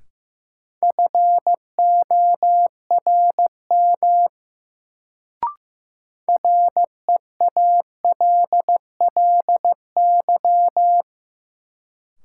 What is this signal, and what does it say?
Morse code, heard as a keyed tone, about 15 wpm, 700 Hz. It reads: FORM REALLY
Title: Sets of 4 Words - New Word Drill - 15wpm